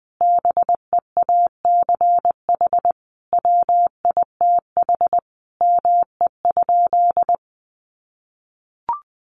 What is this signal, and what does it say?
Morse code, heard as a keyed tone, about 19 wpm, 700 Hz. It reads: BEACH WITH ME?
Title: Sentences from Top 1500 Words 20wpm